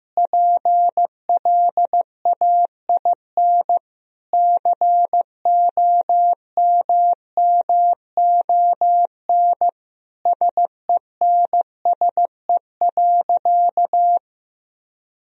Morse code, heard as a keyed tone, about 15 wpm, 700 Hz. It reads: PLAIN COMMON SENSE.
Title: Sentences from Top 1000 Words - Review Part 4 - 15wpm